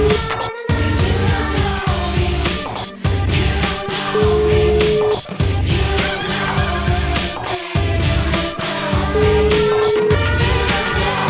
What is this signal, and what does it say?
Ringback tone (Europe, 425 Hz), with noise.